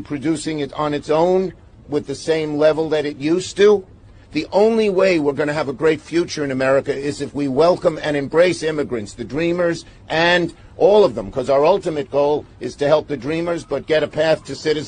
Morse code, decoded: none